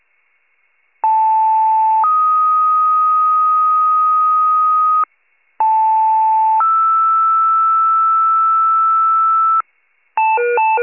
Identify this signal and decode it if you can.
Two-tone page, heard as a one-second tone, then a three-second tone, about 880 then 1250 Hz, about 880 then 1320 Hz. It is heard from 1.0 to 9.6 s.